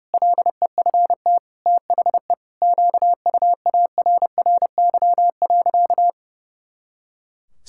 Morse code, heard as a keyed tone, about 30 wpm, 700 Hz. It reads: LEFT THE QUARRY.
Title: Pangram Challenge v2 - 30wpm